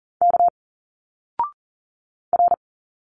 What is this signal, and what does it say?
Morse code, heard as a keyed tone, about 40 wpm, 700 Hz. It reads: K R
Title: Single Letters 40wpm